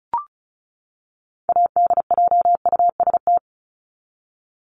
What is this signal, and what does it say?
Morse code, heard as a keyed tone, about 34 wpm, 700 Hz. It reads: ADJUST
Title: Top 1501-1750 Words 35wpm